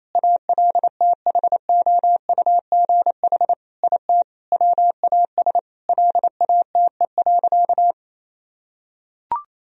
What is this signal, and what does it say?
Morse code, heard as a keyed tone, about 28 wpm, 700 Hz. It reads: ALTHOUGH IT WAS LATE.
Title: Sentences from Top 1000 Words - Review Part 3 - 28wpm